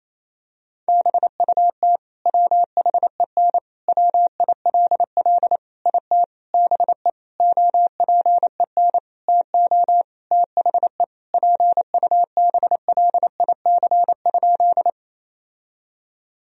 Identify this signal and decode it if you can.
Morse code, heard as a keyed tone, about 28 wpm, 700 Hz. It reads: BUT WHEN WILL IT BE OPEN TO THE PUBLIC?
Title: Sentences from Top 1000 Words - Review Part 3 - 28wpm